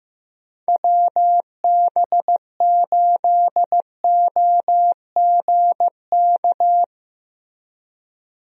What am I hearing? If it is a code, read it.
Morse code, heard as a keyed tone, about 15 wpm, 700 Hz. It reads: WB8OGK
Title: USA Call Signs - 6 Characters - Repeated 3x - 15wpm